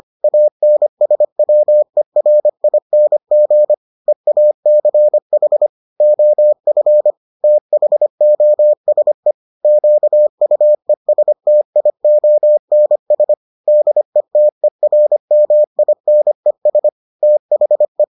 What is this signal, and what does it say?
Morse code, heard as a keyed tone, about 25 wpm, 600 Hz. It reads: ANSWERING EACH OF THOSE QUESTIONS DETERMINES THE